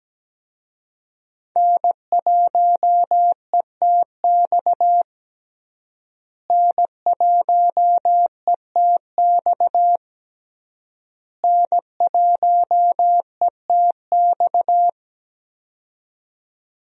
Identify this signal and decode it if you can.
Morse code, heard as a keyed tone, about 17 wpm, 700 Hz. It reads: N1ETX N1ETX N1ETX
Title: USA Call Signs - 5 Characters - Repeated 3x - 17wpm